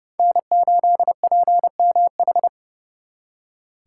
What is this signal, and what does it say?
Morse code, heard as a keyed tone, about 37 wpm, 700 Hz. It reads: N8PMH
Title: Speed Racing - USA 1x3 Call Signs 30wpm